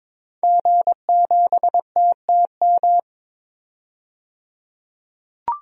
Morse code, heard as a keyed tone, about 22 wpm, 700 Hz. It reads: G7TTM
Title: DXCC-100 Call Signs - 5 Characters - 22wpm